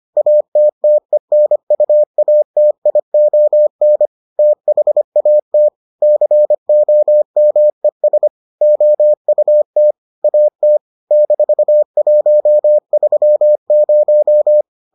Morse code, heard as a keyed tone, about 25 wpm, 600 Hz. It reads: ATTENUATION THAT COMES OUT AT -130